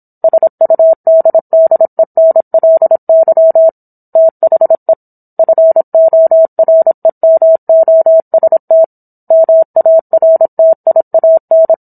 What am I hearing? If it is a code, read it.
Morse code, heard as a keyed tone, about 27 wpm, 650 Hz. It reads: SUDDENLY THE FOREMOST MARTIAN